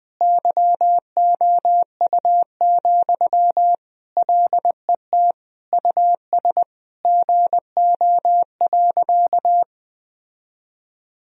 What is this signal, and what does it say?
Morse code, heard as a keyed tone, about 20 wpm, 700 Hz. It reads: YOU, LET US GO.
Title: Sentences from Top 1000 Words - Review Part 2 - 20wpm